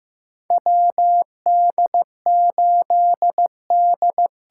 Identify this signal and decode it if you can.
Morse code, heard as a keyed tone, about 15 wpm, 700 Hz. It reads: WD8D
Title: USA Call Signs - 6 Characters - 8x Character Spacing - 15wpm